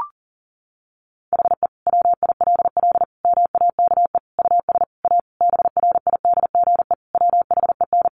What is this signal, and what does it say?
Morse code, heard as a keyed tone, about 37 wpm, 700 Hz. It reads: HE WILL MAKE US A BRIDGE WHEN
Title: Sentences from Top 1200 Words 40wpm